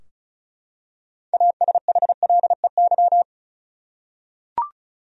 Morse code, heard as a keyed tone, about 36 wpm, 700 Hz. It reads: ASHLEY
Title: USA Common Names - 35wpm